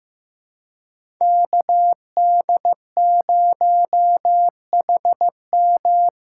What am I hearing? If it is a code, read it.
Morse code, heard as a keyed tone, about 15 wpm, 700 Hz. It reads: KD0HM
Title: USA Call Signs - 5 Characters - 4x Character Spacing - 15wpm